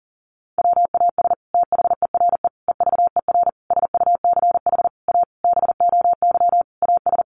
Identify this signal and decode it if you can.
Morse code, heard as a keyed tone, about 40 wpm, 700 Hz. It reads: WAS THERE EVER SUCH A BOY AS
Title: Sentences from Top 400 Words 40wpm